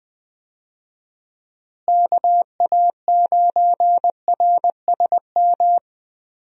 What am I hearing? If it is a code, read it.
Morse code, heard as a keyed tone, about 20 wpm, 700 Hz. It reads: KA9RSM
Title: USA Call Signs - 6 Characters - Repeated 3x - 20wpm